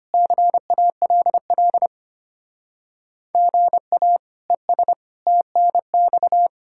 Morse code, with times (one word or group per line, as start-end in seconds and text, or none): 0.1-1.9 CALL
3.3-4.2 GA
4.5-4.9 ES
5.3-6.5 TNX